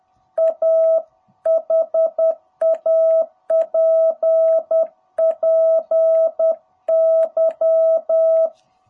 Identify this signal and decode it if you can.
Morse code, heard as a keyed tone, about 10 wpm, 660 Hz. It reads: A HAPPY
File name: jh8cbh / 2021cw